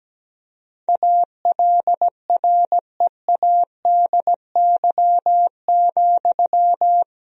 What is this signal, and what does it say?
Morse code, heard as a keyed tone, about 17 wpm, 700 Hz. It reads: ALREADY,
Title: Sentences from Top 1000 Words - Review Part 4 - 17wpm